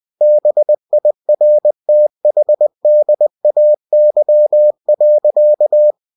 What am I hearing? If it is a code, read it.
Morse code, heard as a keyed tone, about 20 wpm, 600 Hz. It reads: BIRTHDAY.
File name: cw-books / winnie-the-pooh / winnie-20wpm-page-0051